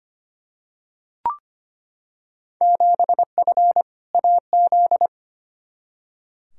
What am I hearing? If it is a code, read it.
Morse code, heard as a keyed tone, about 25 wpm, 700 Hz. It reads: 7F AZ
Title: Field Day Exchanges Encore - 25wpm